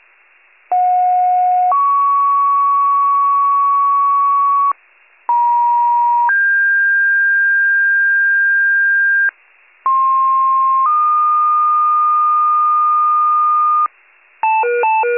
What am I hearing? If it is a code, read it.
Two-tone page, heard as a one-second tone, then a three-second tone, about 730 then 1120 Hz, about 950 then 1600 Hz, about 1060 then 1220 Hz. It is heard from 0.7 to 13.9 s.